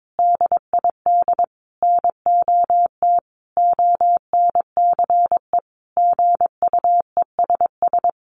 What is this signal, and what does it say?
Morse code, heard as a keyed tone, about 22 wpm, 700 Hz. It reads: DID NOT ONCE GUESS
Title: Sentences from Top 1500 Words 22wpm